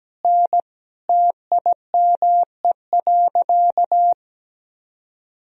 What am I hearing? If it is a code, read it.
Morse code, heard as a keyed tone, about 17 wpm, 700 Hz. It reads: N TIME.
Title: Sentences from Top 100 Words and Abbreviations 17wpm